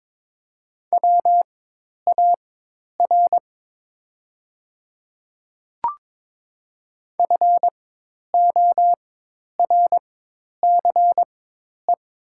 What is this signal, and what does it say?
Morse code, heard as a keyed tone, about 21 wpm, spaced as for 10 wpm, 690 Hz. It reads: WAR FORCE
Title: 4x Character Spacing - Top 500 Words - 22wpm